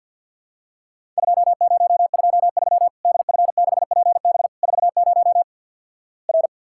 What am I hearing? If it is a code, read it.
Morse code, heard as a keyed tone, about 50 wpm, 700 Hz. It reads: 2023 DF6PB 40 R
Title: Worked All Europe DX Contest - QTC Exchange with Repeats - 50wpm